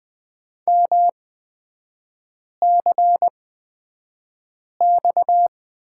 Morse code, heard as a keyed tone, about 20 wpm, spaced as for 5 wpm, 690 Hz. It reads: MCX
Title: 229 Lesson 37e - Letter x - Three Characters 20wpm